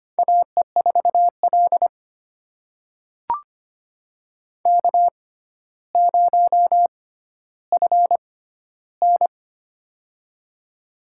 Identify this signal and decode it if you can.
Morse code, heard as a keyed tone, about 25 wpm, spaced as for 13 wpm, 700 Hz. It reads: AE4L K0FN
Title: USA Call Signs - 4 Characters - 6x Character Spacing - 25wpm